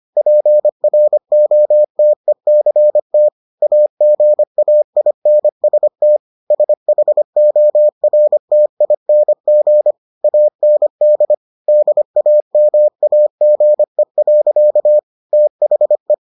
Morse code, heard as a keyed tone, about 25 wpm, 600 Hz. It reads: PROTECT AGAINST SHORTING AND DAMAGE. THE